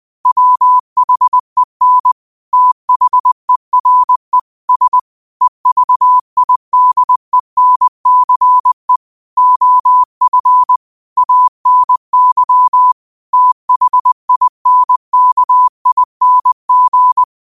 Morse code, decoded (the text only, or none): WHEN THERE S EVIDENCE OF ANY THINKING